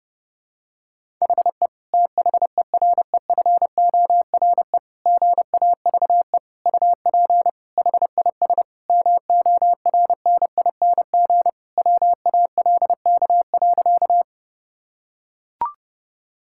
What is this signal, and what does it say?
Morse code, heard as a keyed tone, about 30 wpm, 700 Hz. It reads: HE THEREFORE GAVE UP HIS MORNING WALK.
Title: Sentences from Top 1000 Words - Review Part 5 - 30wpm